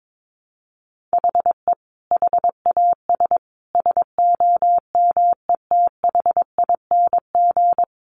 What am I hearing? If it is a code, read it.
Morse code, heard as a keyed tone, about 22 wpm, 700 Hz. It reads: HE HAS SOMETHING
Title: Sentences from Top 1200 Words 22wpm